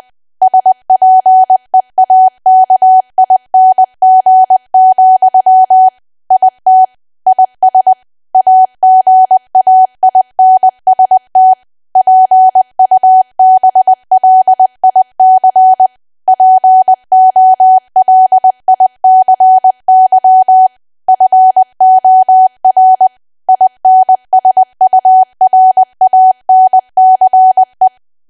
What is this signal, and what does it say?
Morse code, heard as a keyed tone, about 20 wpm, 750 Hz. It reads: SPEAKING, IT IS AGAINST PUBLIC POLICY FOR INSURANCE
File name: cwa / W1AW-Level3 / 20WPM